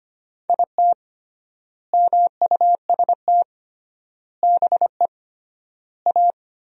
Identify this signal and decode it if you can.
Morse code, heard as a keyed tone, about 25 wpm, 700 Hz. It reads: IT MUST BE A
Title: 3x Word-Spacing - Sentences from Top 500 Words - 25wpm